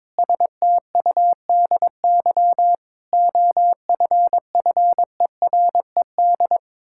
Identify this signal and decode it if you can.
Morse code, heard as a keyed tone, about 22 wpm, 700 Hz. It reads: STUDY OFFERED
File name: practice / 22wpm / Sentences with 8-Word Phrases - 22wpm